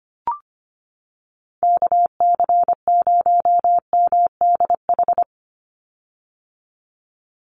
Morse code, heard as a keyed tone, about 25 wpm, 700 Hz. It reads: KC0MDH
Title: USA Call Signs 25wpm